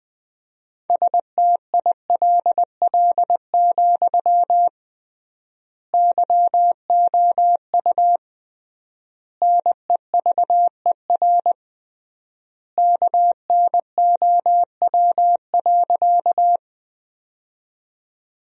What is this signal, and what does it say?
Morse code, heard as a keyed tone, about 20 wpm, 700 Hz. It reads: STILL, YOU NEVER KNOW.
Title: 3x Word-Spacing - Sentences from Top 300 Words - 20wpm